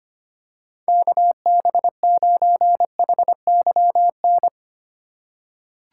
Morse code, heard as a keyed tone, about 25 wpm, 700 Hz. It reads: KB9HYN
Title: USA Call Signs - 6 Characters - 4x Character Spacing - 25wpm